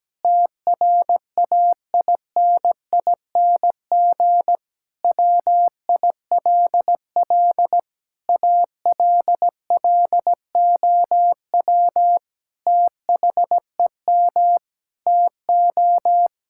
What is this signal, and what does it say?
Morse code, heard as a keyed tone, about 17 wpm, 700 Hz. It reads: TRAINING WILL ALLOW THEM TO
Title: Sentences from Top 1000 Words - Review Part 5 - 17wpm